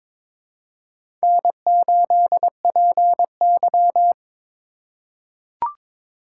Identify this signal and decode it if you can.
Morse code, heard as a keyed tone, about 22 wpm, 700 Hz. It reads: N8PY
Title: USA Call Signs - 4 Characters - 8x Character Spacing - 22wpm